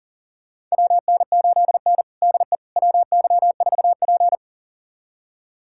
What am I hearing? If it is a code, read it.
Morse code, heard as a keyed tone, about 40 wpm, 700 Hz. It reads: WN8N DE WY4P
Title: USA Callsign DE Callsign - 4 Characters - 40wpm